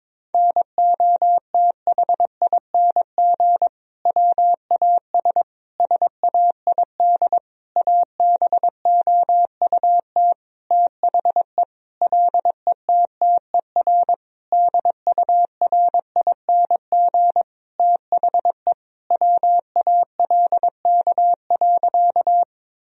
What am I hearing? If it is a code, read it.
Morse code, heard as a keyed tone, about 22 wpm, 700 Hz. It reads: NOTHING WAS SAID ABOUT THE LETTER DURING THE WALK.
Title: Sentences from Top 1000 Words - Review Part 2 - 22wpm